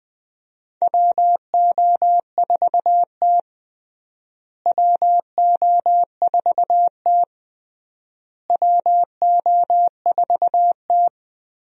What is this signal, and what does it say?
Morse code, heard as a keyed tone, about 21 wpm, 700 Hz. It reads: WO4T WO4T WO4T
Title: USA Call Signs - 4 Characters - Repeated 3x - 20wpm